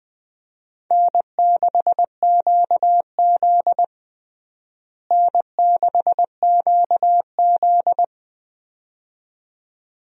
Morse code, heard as a keyed tone, about 21 wpm, 700 Hz. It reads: N6QZ N6QZ
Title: USA Call Signs - 4 Characters - Repeated 2x - 20wpm